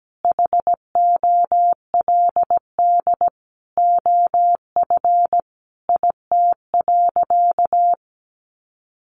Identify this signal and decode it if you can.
Morse code, heard as a keyed tone, about 17 wpm, 700 Hz. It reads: HOLD OF IT.
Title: Sentences from Top 1500 Words 17wpm